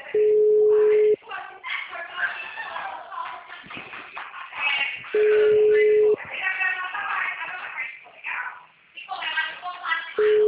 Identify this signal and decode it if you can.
Ringback tone (Europe, 425 Hz), with speech.